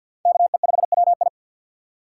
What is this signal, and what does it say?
Morse code, heard as a keyed tone, about 50 wpm, 700 Hz. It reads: KE5WI
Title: USA Call Signs - 5 Characters - 3x Character Spacing - 50wpm